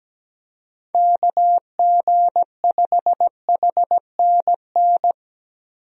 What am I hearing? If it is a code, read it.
Morse code, heard as a keyed tone, about 17 wpm, 700 Hz. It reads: KG5HNN
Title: USA Call Signs - 6 Characters - 17wpm